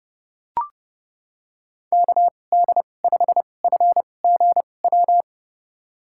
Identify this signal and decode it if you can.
Morse code, heard as a keyed tone, about 30 wpm, spaced as for 22 wpm, 700 Hz. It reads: KD5FGW
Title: USA Call Signs - 2x Character Spacing - 30wpm